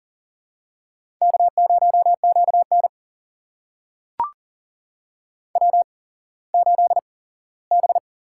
Morse code, heard as a keyed tone, about 40 wpm, spaced as for 19 wpm, 700 Hz. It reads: K0QN W8B
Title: USA Call Signs - 4 Characters - 8x Character Spacing - 40wpm